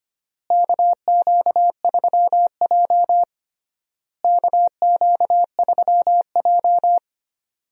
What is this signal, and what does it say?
Morse code, heard as a keyed tone, about 25 wpm, 700 Hz. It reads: KQ3J KQ3J
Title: USA Call Signs - 4 Characters - Repeated 3x - 25wpm